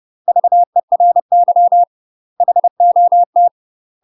Morse code, heard as a keyed tone, about 30 wpm, 700 Hz. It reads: VERY HOT
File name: UK MASTER QSOs TRAINING / 30WPM / UK MASTER QSOs 0043